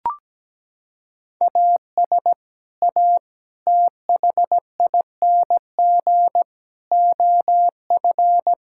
Morse code, heard as a keyed tone, about 17 wpm, 700 Hz. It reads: AS A THING OF
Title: Sentences from Top 1000 Words - Review Part 6 - 17wpm